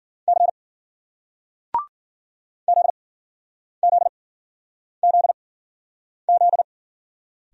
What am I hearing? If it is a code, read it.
Morse code, heard as a keyed tone, about 54 wpm, spaced as for 9 wpm, 700 Hz. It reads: C ZZZZ